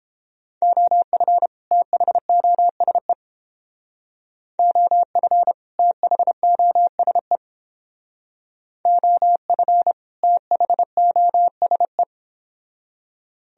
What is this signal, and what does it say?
Morse code, heard as a keyed tone, about 30 wpm, 700 Hz. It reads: OF THOSE OF THOSE OF THOSE